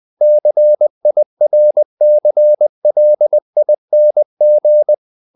Morse code, heard as a keyed tone, about 20 wpm, 600 Hz. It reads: CIRCLING
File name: the old man and the sea / the old man and the sea part 1 20wpm